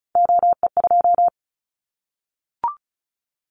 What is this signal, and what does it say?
Morse code, heard as a keyed tone, about 36 wpm, 700 Hz. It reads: OE2
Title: Callsign Prefixes for Top 100 DXCC - 35wpm